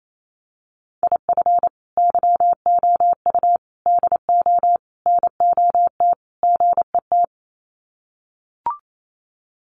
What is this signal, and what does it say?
Morse code, heard as a keyed tone, about 28 wpm, 700 Hz. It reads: IF YOU DO NOT GET
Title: Sets of 5 Words Encore2 28wpm